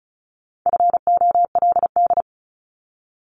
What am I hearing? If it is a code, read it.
Morse code, heard as a keyed tone, about 34 wpm, 700 Hz. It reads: FOLD
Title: Top 1751-2000 Words 35wpm